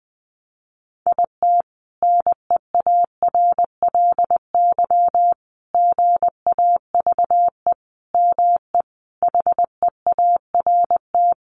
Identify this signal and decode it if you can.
Morse code, heard as a keyed tone, about 19 wpm, 700 Hz. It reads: IT NEARLY GAVE ME HEART
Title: Sentences from Top 1300 Words 20wpm